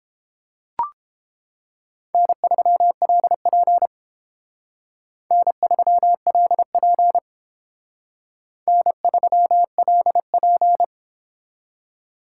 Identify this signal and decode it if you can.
Morse code, heard as a keyed tone, about 28 wpm, 700 Hz. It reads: N3LP N3LP N3LP